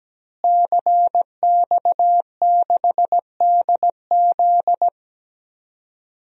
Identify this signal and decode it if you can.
Morse code, heard as a keyed tone, about 17 wpm, 700 Hz. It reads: CX6DZ
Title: DXCC-100 Call Signs - 5 Characters - Repeated 3x - 17wpm